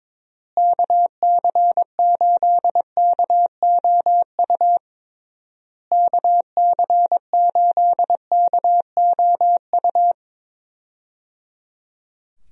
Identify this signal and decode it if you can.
Morse code, heard as a keyed tone, about 21 wpm, 700 Hz. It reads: KC8KOU KC8KOU
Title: USA Call Signs - 6 Characters - Repeated 3x - 22wpm